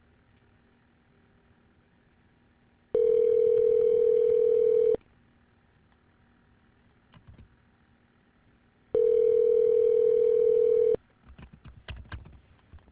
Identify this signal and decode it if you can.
Ringback tone (North America), 440 + 480 Hz.